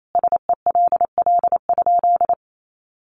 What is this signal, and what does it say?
Morse code, heard as a keyed tone, about 28 wpm, 700 Hz. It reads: SELL?
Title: Sentences from Top 1200 Words 28wpm